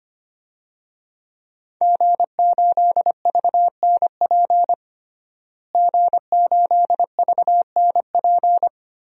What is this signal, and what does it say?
Morse code, heard as a keyed tone, about 25 wpm, 700 Hz. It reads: G8VNP G8VNP
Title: DXCC-100 Call Signs - 5 Characters - Repeated 3x - 25wpm